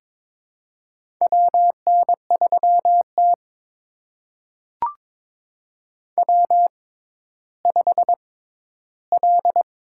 Morse code, heard as a keyed tone, about 22 wpm, spaced as for 13 wpm, 700 Hz. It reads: WN3T W5L
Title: USA Call Signs - 4 Characters - 6x Character Spacing - 22wpm